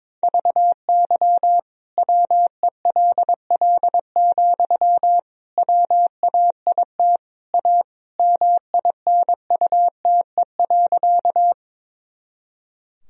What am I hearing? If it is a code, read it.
Morse code, heard as a keyed tone, about 22 wpm, 700 Hz. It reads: VY WELL, WAIT A MINUTE.
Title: Sentences from Top 500 Words and Abbreviations 22wpm